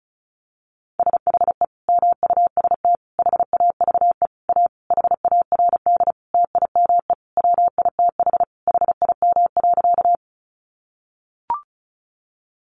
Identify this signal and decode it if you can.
Morse code, heard as a keyed tone, about 34 wpm, 700 Hz. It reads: SHE MUST HAVE A HARD TIME WITH HIM.
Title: Sentences from Top 300 Words 35wpm